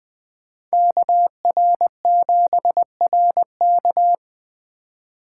Morse code, heard as a keyed tone, about 20 wpm, 700 Hz. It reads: KR7RK
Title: 2023 SOTA Activator Call Signs - 20wpm